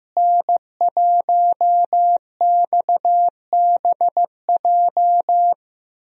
Morse code, heard as a keyed tone, about 15 wpm, 700 Hz. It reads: N1XBJ
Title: USA Call Signs - 5 Characters - Repeated 2x - 15wpm